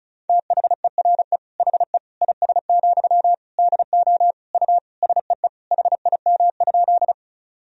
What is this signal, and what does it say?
Morse code, heard as a keyed tone, about 34 wpm, 700 Hz. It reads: THERE HE IS, DO U SEE HIM?
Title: Sentences from Top 100 Words and Abbreviations 35wpm